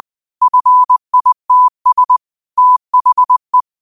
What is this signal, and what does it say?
Morse code, heard as a keyed tone, about 20 wpm, 1000 Hz. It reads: FITS THE
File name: qotdfiles / 20wpm